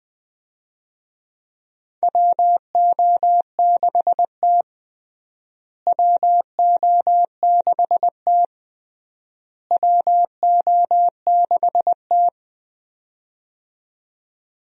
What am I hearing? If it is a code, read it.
Morse code, heard as a keyed tone, about 20 wpm, 700 Hz. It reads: WO6T WO6T WO6T
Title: USA Call Signs - 4 Characters - Repeated 3x - 20wpm